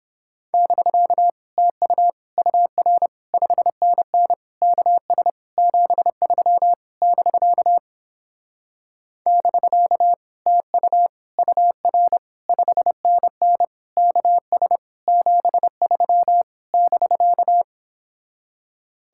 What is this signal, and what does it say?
Morse code, heard as a keyed tone, about 29 wpm, 700 Hz. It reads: <BK> TU UR 5NN KS 73 <BK> <BK> TU UR 5NN KS 73 <BK>